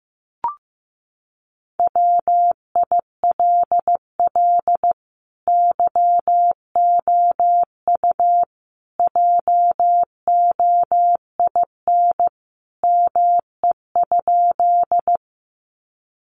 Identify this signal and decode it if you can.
Morse code, heard as a keyed tone, about 15 wpm, 700 Hz. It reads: WILL YOU JOIN ME?
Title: Sentences from Top 1400 Words 15wpm